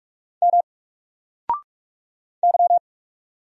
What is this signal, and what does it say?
Morse code, heard as a keyed tone, about 45 wpm, 700 Hz. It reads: M Y
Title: Single Character (Letter:Number:Punctuation:Prosign) Alt 45wpm